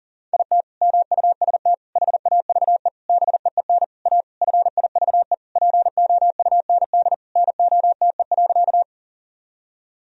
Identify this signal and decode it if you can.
Morse code, heard as a keyed tone, about 40 wpm, 700 Hz. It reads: IT MUST HAVE BEEN A FIVE POUND NOTE.
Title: Sentences from Top 1000 Words - Review Part 2 - 40wpm